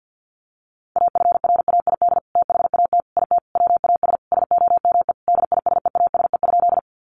Morse code, heard as a keyed tone, about 50 wpm, 700 Hz. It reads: AFRAID THAT IT WAS SOME DISEASE?
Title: Sentences from Top 1500 Words 50wpm